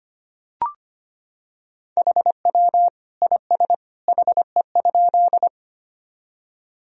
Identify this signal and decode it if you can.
Morse code, heard as a keyed tone, about 25 wpm, 700 Hz. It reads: HW IS HE?
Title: Sentences from Top 100 Words and Abbreviations 25wpm